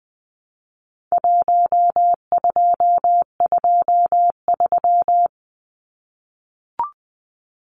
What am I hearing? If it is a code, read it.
Morse code, heard as a keyed tone, about 20 wpm, 700 Hz. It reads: 1223
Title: Numbers - Contest Style - 4-Digit 20wpm